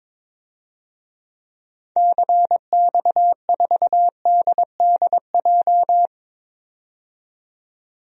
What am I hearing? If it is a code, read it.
Morse code, heard as a keyed tone, about 22 wpm, 700 Hz. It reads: CX4DDJ
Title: DXCC-100 Call Signs - 6 Characters - 22wpm